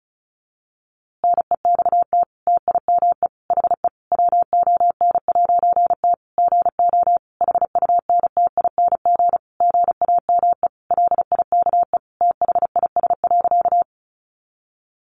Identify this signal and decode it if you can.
Morse code, heard as a keyed tone, about 34 wpm, 700 Hz. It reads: NEXT TIME HE WON'T GO HUNTING GAME LIKE THIS.
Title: A Sound Of Thunder 35wpm